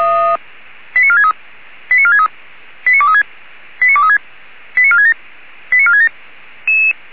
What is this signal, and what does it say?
Five-tone selective call (ZVEI), heard as quick runs of five tones, tones of about 80 ms. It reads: 89463 89463 89236 89236 89467 89467
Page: SDR/ZVEI